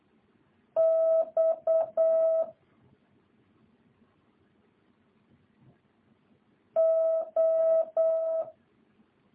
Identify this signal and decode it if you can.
Morse code, heard as a keyed tone, about 8 wpm, 650 Hz. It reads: X O